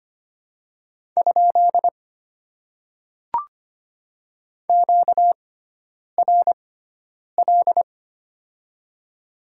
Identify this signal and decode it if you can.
Morse code, heard as a keyed tone, about 25 wpm, spaced as for 9 wpm, 700 Hz. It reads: ? QRL
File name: Top 90 QSO Elements - 6x Character Spacing - 25wpm